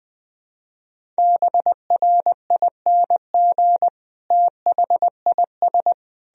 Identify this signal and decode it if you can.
Morse code, heard as a keyed tone, about 20 wpm, 700 Hz. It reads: BRING THIS
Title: Sentences from Top 500 Words and Abbreviations 20wpm